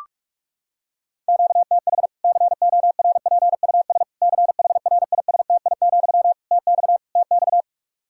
Morse code, heard as a keyed tone, about 46 wpm, 700 Hz. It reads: QTH CORPUS CHRISTI, TX TX
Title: Practice QSO - Exchange 1 - Beginning - 45wpm